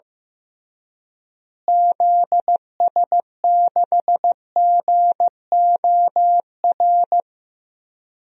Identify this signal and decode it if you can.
Morse code, heard as a keyed tone, about 15 wpm, 700 Hz. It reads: ZS6GOR
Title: DXCC-100 Call Signs - 6 Characters - Repeated 3x - 15wpm